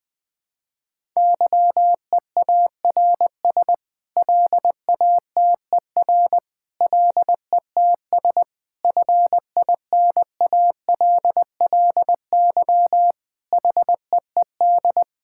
Morse code, heard as a keyed tone, about 20 wpm, 700 Hz. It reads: YEARS LATER LETS FINALLY HEED
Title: Sets of 8 Words - 20wpm